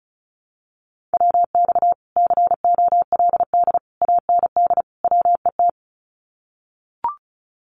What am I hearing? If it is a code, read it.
Morse code, heard as a keyed tone, about 34 wpm, 700 Hz. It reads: WX COLD AND WET
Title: Common QSO Phrases Encore 35wpm